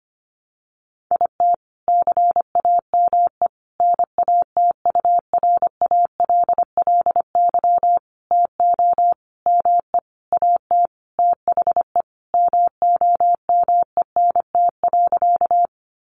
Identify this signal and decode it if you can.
Morse code, heard as a keyed tone, about 25 wpm, 700 Hz. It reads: IT CAME NATURALLY TO ME AT THE MOMENT.
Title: Sentences from Top 1300 Words 25wpm